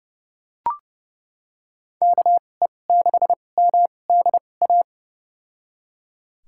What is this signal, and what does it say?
Morse code, heard as a keyed tone, about 30 wpm, spaced as for 22 wpm, 700 Hz. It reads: KE6MDA
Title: USA Call Signs - 6 Characters - 2x Character Spacing - 30wpm